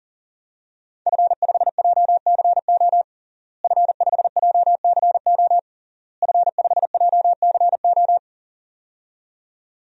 Morse code, heard as a keyed tone, about 40 wpm, 700 Hz. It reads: F5JCO F5JCO F5JCO
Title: DXCC-100 Call Signs - 5 Characters - Repeated 3x - 40wpm